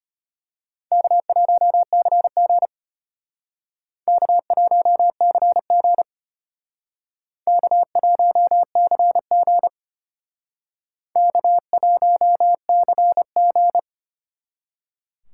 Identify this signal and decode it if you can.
Morse code, heard as a keyed tone, about 32 wpm, 700 Hz. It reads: K1CG K1CG K1CG K1CG